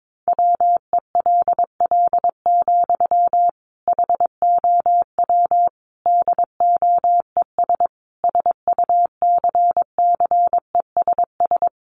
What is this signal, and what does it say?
Morse code, heard as a keyed tone, about 22 wpm, 700 Hz. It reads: WELL, HOW DOES SUCCESS
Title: Sentences from Top 1100 Words 22wpm